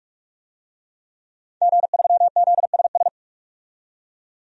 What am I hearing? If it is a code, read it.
Morse code, heard as a keyed tone, about 48 wpm, 700 Hz. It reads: G3ZSS
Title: DXCC-100 Call Signs - 5 Characters - 45wpm